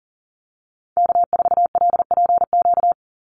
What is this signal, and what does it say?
Morse code, heard as a keyed tone, about 40 wpm, 700 Hz. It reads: K4LPQ
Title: NA SOTA Activator Callsigns 40wpm